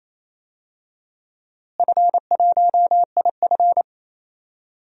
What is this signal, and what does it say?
Morse code, heard as a keyed tone, about 28 wpm, 700 Hz. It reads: F1IF
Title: DXCC-100 Call Signs - 4 Characters - Repeated 3x - 28wpm